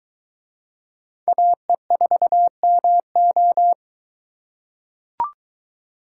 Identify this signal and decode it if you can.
Morse code, heard as a keyed tone, about 23 wpm, 700 Hz. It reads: AE4MO